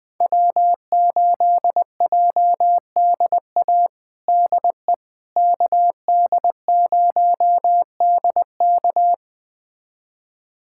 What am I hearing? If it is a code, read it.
Morse code, heard as a keyed tone, about 21 wpm, 700 Hz. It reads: W8JDA DE KD0DK
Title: USA Callsign DE Callsign - 5 Characters - 20wpm